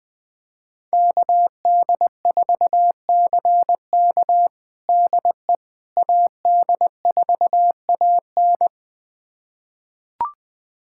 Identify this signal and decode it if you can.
Morse code, heard as a keyed tone, about 20 wpm, 700 Hz. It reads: KD4CK DE AD4AN
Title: USA Callsign DE Callsign - 5 Characters - 20wpm